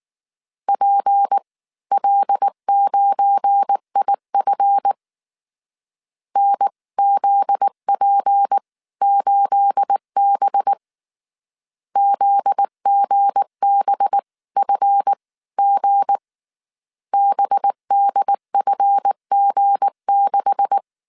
Morse code, heard as a keyed tone, about 19 wpm, spaced as for 16 wpm, 800 Hz. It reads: PL9IF NZP8B ZGBFG BDFG6
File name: CW / DET / 1 9 1500C S A 18wpm 2459